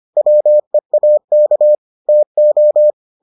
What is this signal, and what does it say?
Morse code, heard as a keyed tone, about 25 wpm, 600 Hz. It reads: WEAK TO